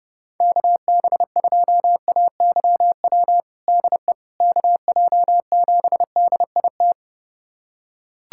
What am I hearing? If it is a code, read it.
Morse code, heard as a keyed tone, about 30 wpm, 700 Hz. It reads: KB2AYW DE KJ7DIT
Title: USA Callsign DE Callsign - 6 Characters - 30wpm